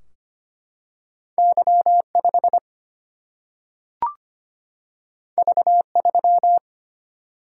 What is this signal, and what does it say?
Morse code, heard as a keyed tone, about 25 wpm, 700 Hz. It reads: Y 5 V 3
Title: Single-Letter Single-Number 25wpm